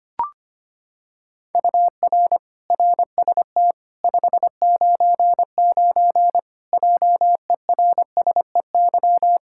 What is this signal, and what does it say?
Morse code, heard as a keyed tone, about 25 wpm, 700 Hz. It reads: UR RST 599 JERSEY
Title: Practice QSO - Exchange 1 - Beginning - 25wpm